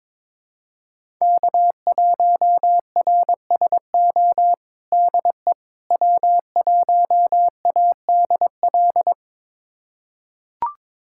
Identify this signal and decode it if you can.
Morse code, heard as a keyed tone, about 22 wpm, 700 Hz. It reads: K1RSO DE W1ADL
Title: USA Callsign DE Callsign - 5 Characters - 22wpm